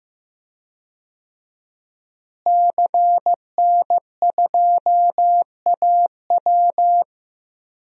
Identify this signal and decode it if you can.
Morse code, heard as a keyed tone, about 15 wpm, 700 Hz. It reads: CN2AW
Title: DXCC-100 Call Signs - 5 Characters - Repeated 2x - 15wpm